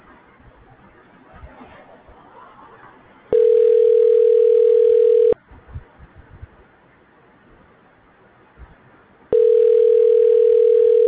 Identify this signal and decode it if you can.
Ringback tone (North America), 440 + 480 Hz.